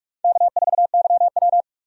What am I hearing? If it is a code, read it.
Morse code, heard as a keyed tone, about 42 wpm, 700 Hz. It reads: K4YW
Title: USA Call Signs - 4 Characters - 45wpm